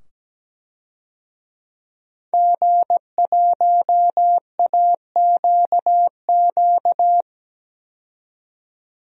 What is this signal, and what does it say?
Morse code, heard as a keyed tone, about 17 wpm, 700 Hz. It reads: G1AQQ
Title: DXCC-100 Call Signs - 5 Characters - 17wpm